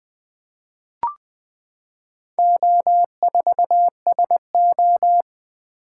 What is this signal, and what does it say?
Morse code, heard as a keyed tone, about 20 wpm, 700 Hz. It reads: O4SO
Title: Four Letter-Number Combinations - 20wpm